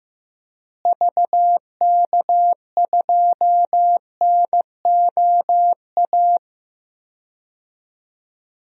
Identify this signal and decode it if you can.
Morse code, heard as a keyed tone, about 15 wpm, 700 Hz. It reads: VK2NOA
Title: DXCC-100 Call Signs - 6 Characters - Repeated 2x - 15wpm